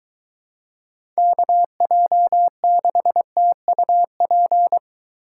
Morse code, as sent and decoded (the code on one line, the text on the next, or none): -.- .--- -.... - ..- .--.
KJ6TUP